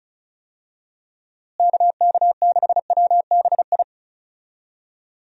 Morse code, heard as a keyed tone, about 36 wpm, 700 Hz. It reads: KK6WBI
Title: USA Call Signs - 6 Characters - Repeated 3x - 35wpm